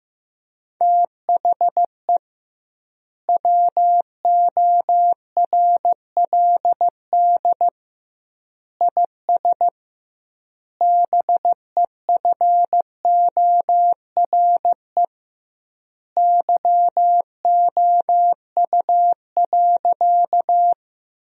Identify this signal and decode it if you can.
Morse code, heard as a keyed tone, about 15 wpm, 700 Hz. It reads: THE WORLD IS BEFORE YOU.
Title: 2x Word-Spacing - Sentences from Top 200 Words - 15wpm